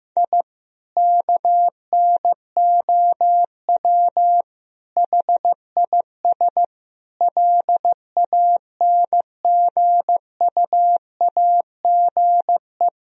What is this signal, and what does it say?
Morse code, heard as a keyed tone, about 15 wpm, 700 Hz. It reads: I KNOW HIS LANGUAGE
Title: Sentences from Top 1000 Words - Review Part 3 - 15wpm